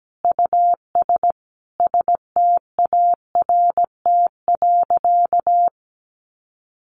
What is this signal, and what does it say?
Morse code, heard as a keyed tone, about 17 wpm, 700 Hz. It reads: US START.
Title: Sentences from Top 700 Words 17wpm